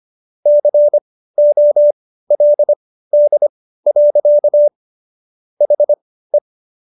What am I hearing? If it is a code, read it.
Morse code, heard as a keyed tone, about 25 wpm, spaced as for 15 wpm, 590 Hz. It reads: COLD. HE